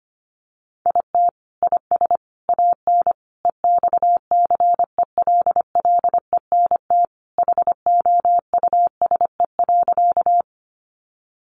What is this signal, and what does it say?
Morse code, heard as a keyed tone, about 25 wpm, 700 Hz. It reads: IT IS AN EXCELLENT HOUSE.
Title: Sentences from Top 1000 Words 25wpm